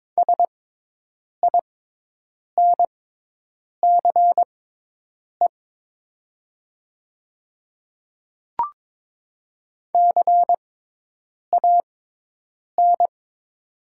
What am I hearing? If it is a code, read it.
Morse code, heard as a keyed tone, about 22 wpm, spaced as for 8 wpm, 700 Hz. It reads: SINCE CAN
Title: 6x Character Spacing - Top 300 Words - 22wpm